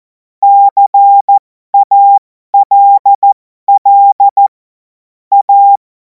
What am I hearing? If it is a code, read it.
Morse code, heard as a keyed tone, about 14 wpm, spaced as for 12 wpm, 800 Hz. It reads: CALL A
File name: Morsekurs / Witze